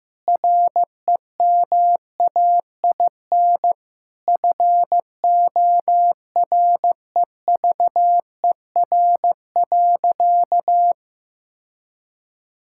Morse code, heard as a keyed tone, about 16 wpm, 700 Hz. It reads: REMAIN FOREVER.
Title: Sentences from Top 1600 Words 15wpm